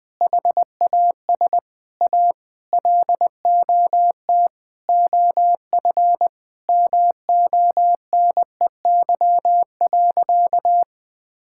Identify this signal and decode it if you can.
Morse code, heard as a keyed tone, about 20 wpm, 700 Hz. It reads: HAS A LOT OF MONEY.
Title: Sentences from Top 1000 Words - Review Part 5 - 20wpm